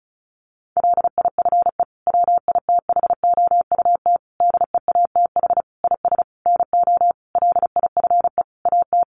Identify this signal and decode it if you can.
Morse code, heard as a keyed tone, about 34 wpm, 700 Hz. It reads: LIFE WITHOUT DEATH IS NO LIFE AT
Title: Sentences from Top 1100 Words 35wpm